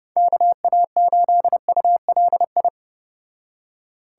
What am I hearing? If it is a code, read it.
Morse code, heard as a keyed tone, about 30 wpm, 700 Hz. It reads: KA8ULI